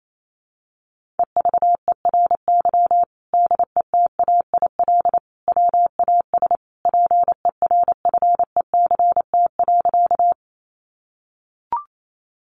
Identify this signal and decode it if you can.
Morse code, heard as a keyed tone, about 28 wpm, 700 Hz. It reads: EVERY DETAIL WAS PERFECT.
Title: Sentences from Top 1500 Words 28wpm